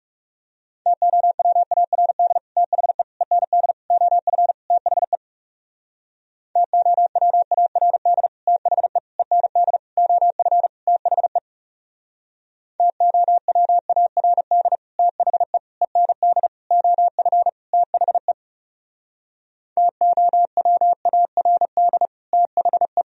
Morse code, heard as a keyed tone, about 37 wpm, 700 Hz. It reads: TOWARD THE END OF THE TOWARD THE END OF THE TOWARD THE END OF THE TOWARD THE
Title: Speed-Racing - Sets of 5 Words 30wpm